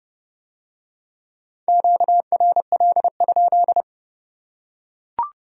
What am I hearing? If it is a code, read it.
Morse code, heard as a keyed tone, about 29 wpm, 700 Hz. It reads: QRL?